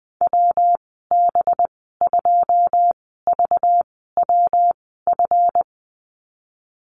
Morse code, heard as a keyed tone, about 20 wpm, spaced as for 15 wpm, 700 Hz. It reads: WB2VWF
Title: 238 Lesson 38g - Letter q - Call Signs 20wpm